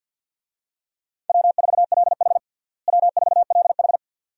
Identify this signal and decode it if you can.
Morse code, heard as a keyed tone, about 51 wpm, 700 Hz. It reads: W4LH W4LH
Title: USA Call Signs - 4 Characters - Repeated 2x - 50wpm